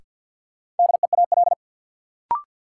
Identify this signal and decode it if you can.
Morse code, heard as a keyed tone, about 53 wpm, 700 Hz. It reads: DEAL
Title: Sentences from Top 800 Words - New Word Drill - 50wpm